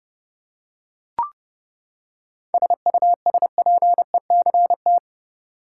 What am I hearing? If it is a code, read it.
Morse code, heard as a keyed tone, about 31 wpm, 700 Hz. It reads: SUSPECT
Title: Seven Letter Words - 30wpm